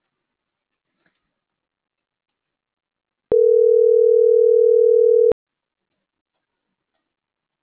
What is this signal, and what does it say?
Ringback tone (North America), 440 + 480 Hz.